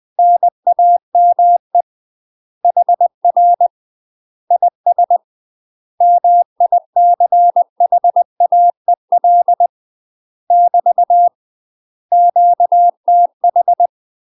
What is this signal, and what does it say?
Morse code, heard as a keyed tone, about 20 wpm, 700 Hz. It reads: NAME HR IS MICHAEL <BT> QTH
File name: UK MASTER QSOs TRAINING / 20WPM / UK MASTER QSOs 0045